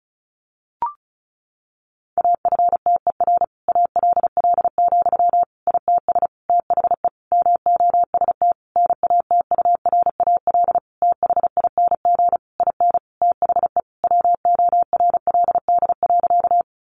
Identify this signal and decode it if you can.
Morse code, heard as a keyed tone, about 34 wpm, 700 Hz. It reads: AFTER ALL, ITS THE MOST NATURAL THING IN THE WORLD.
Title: Sentences from Top 700 Words 35wpm